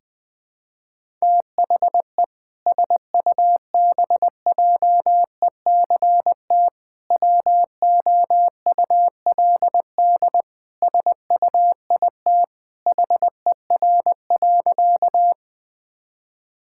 Morse code, heard as a keyed tone, about 21 wpm, 700 Hz. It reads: THE SUBJECT WOULD SUIT HER.
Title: Sentences from Top 1600 Words 20wpm